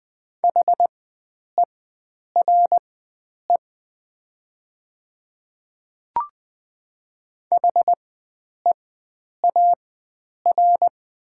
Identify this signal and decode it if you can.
Morse code, heard as a keyed tone, about 20 wpm, spaced as for 9 wpm, 700 Hz. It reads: HERE HEAR
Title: 4x Character Spacing - Top 300 Words - 20wpm